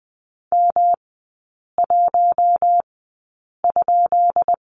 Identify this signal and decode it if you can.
Morse code, heard as a keyed tone, about 20 wpm, spaced as for 8 wpm, 690 Hz. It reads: M1?
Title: 133 Lesson 22e - Question Mark - Three Characters 20wpm